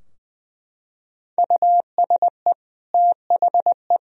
Morse code, heard as a keyed tone, about 20 wpm, 700 Hz. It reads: USE THE
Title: Sentences from Top 1000 Words - Review Part 2 - 20wpm